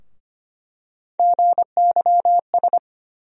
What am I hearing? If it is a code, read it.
Morse code, heard as a keyed tone, about 25 wpm, 700 Hz. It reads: GYS
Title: USA Call Sign Postfixes - 3 Letters 25wpm